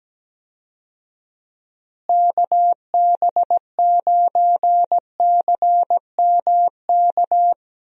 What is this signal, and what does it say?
Morse code, heard as a keyed tone, about 17 wpm, 700 Hz. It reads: KB9CMK
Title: USA Call Signs - 6 Characters - Repeated 2x - 17wpm